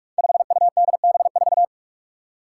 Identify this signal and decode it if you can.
Morse code, heard as a keyed tone, about 44 wpm, 700 Hz. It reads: 5UDB4
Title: Five Challenging Letter-Number Combinations 45wpm